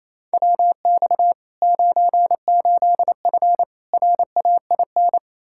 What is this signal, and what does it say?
Morse code, heard as a keyed tone, about 27 wpm, 700 Hz. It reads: WX 98F RAIN
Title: Practice QSO - Exchange 2 - Info - 28wpm